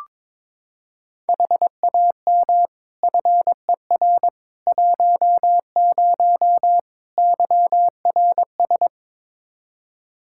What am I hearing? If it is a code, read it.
Morse code, heard as a keyed tone, about 22 wpm, 700 Hz. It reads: HAM FER 10 YRS
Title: Practice QSO - Exchange 3 - More Info - 22wpm